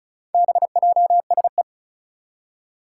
Morse code, heard as a keyed tone, about 36 wpm, 700 Hz. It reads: BJSE
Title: Four Challenging Letter-Number Combinations 35wpm